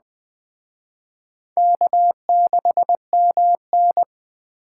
Morse code, heard as a keyed tone, about 20 wpm, 700 Hz. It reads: K6MN
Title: Speed Racing - USA 1x2 Call Signs 17wpm